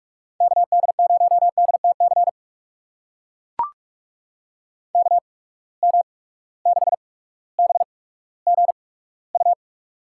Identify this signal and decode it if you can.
Morse code, heard as a keyed tone, about 44 wpm, spaced as for 19 wpm, 700 Hz. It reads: KD0DTC KM6BGU
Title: USA Call Signs - 6 Characters - 8x Character Spacing - 45wpm